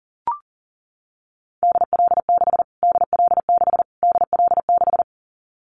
Morse code, heard as a keyed tone, about 40 wpm, 700 Hz. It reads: DL6 DL6 DL6
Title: Callsign Prefixes for Top 25 DXCC - Repeated 3x - 40wpm